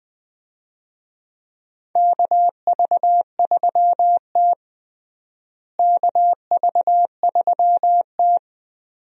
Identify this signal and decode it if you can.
Morse code, heard as a keyed tone, about 20 wpm, 700 Hz. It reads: KV3T KV3T
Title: USA Call Signs - 4 Characters - Repeated 2x - 20wpm